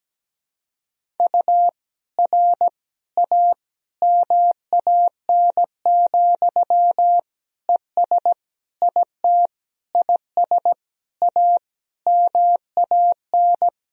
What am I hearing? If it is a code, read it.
Morse code, heard as a keyed tone, about 17 wpm, 700 Hz. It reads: U R A MAN, ES IT IS A MAN
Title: Sentences from Top 200 Words and Abbreviations 17wpm